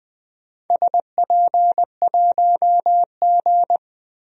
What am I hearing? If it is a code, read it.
Morse code, heard as a keyed tone, about 20 wpm, 700 Hz. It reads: SP1G
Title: DXCC-100 Call Signs - 4 Characters - Repeated 2x - 20wpm